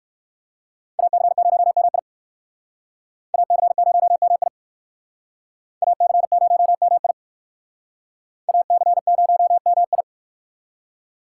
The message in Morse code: AC0MI AC0MI AC0MI AC0MI

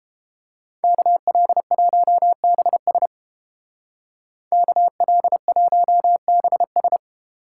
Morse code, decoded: KL1BS KL1BS

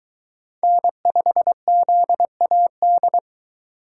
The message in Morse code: N5ZAD

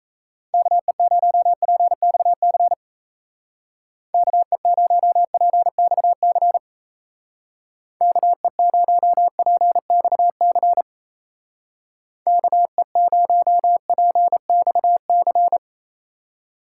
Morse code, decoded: KE0PXC KE0PXC KE0PXC KE0PXC

